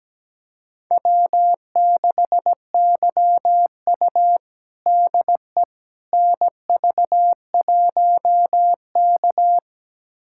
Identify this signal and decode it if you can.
Morse code, heard as a keyed tone, about 17 wpm, 700 Hz. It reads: W6YU DE NV1K
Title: USA Callsign DE Callsign - 4 Characters - 17wpm